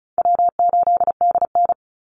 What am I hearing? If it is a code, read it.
Morse code, heard as a keyed tone, about 34 wpm, 700 Hz. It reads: W8DN
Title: Easy Call Signs Repeated 3x - 35wpm